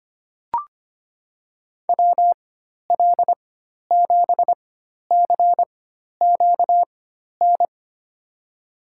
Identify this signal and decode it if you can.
Morse code, heard as a keyed tone, about 25 wpm, spaced as for 12 wpm, 700 Hz. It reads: WL7CQN
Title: USA Call Signs - 4x Character Spacing - 25wpm